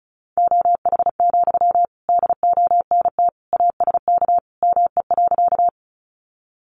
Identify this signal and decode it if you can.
Morse code, heard as a keyed tone, about 34 wpm, 700 Hz. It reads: OH, DONT ASK ME.
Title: Sentences from Top 500 Words 35wpm